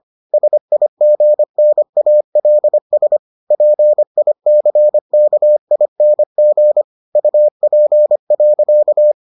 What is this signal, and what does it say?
Morse code, heard as a keyed tone, about 25 wpm, 600 Hz. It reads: SIGNALS PICKING UP.